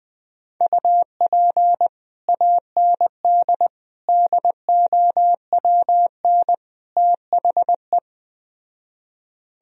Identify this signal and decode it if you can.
Morse code, heard as a keyed tone, about 20 wpm, 700 Hz. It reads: UP AND DOWN THE
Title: Speed-Racing - Sets of 4 Words 17wpm